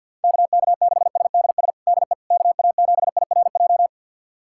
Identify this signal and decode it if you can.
Morse code, heard as a keyed tone, about 51 wpm, 690 Hz. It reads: KK6SDS DE KA7IRJ
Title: USA Callsign DE Callsign - 6 Characters - 50wpm